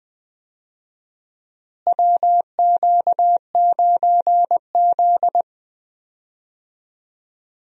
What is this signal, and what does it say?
Morse code, heard as a keyed tone, about 20 wpm, 700 Hz. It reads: WQ9Z
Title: USA Call Signs - 4 Characters - Repeated 3x - 20wpm